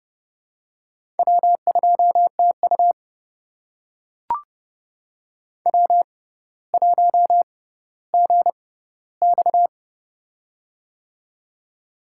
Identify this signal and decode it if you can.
Morse code, heard as a keyed tone, about 30 wpm, spaced as for 15 wpm, 700 Hz. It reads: W2TU W1GX
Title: USA Call Signs - 4 Characters - 6x Character Spacing - 30wpm